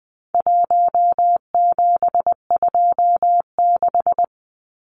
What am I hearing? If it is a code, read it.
Morse code, heard as a keyed tone, about 20 wpm, 700 Hz. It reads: 1726
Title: Numbers - 4-Digit 20wpm